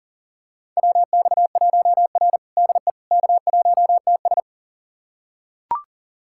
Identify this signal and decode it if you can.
Morse code, heard as a keyed tone, about 37 wpm, 700 Hz. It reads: WX1R DE K1TS
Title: USA Callsign DE Callsign - 4 Characters - 40wpm